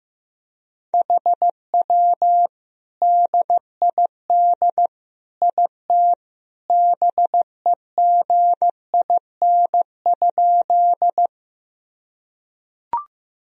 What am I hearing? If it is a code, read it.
Morse code, heard as a keyed tone, about 15 wpm, 700 Hz. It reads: HW DID IT BEGIN?
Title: Sentences from Top 300 Words and Abbreviations 15wpm